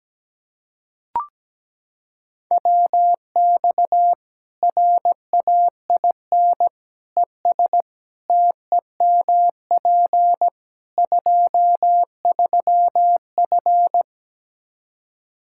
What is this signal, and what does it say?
Morse code, heard as a keyed tone, about 17 wpm, 700 Hz. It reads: WX RAIN ES TEMP 23F
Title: K1USN SST QSO Practice 17wpm